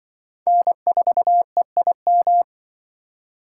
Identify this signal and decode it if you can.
Morse code, heard as a keyed tone, about 26 wpm, 700 Hz. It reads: N4EIM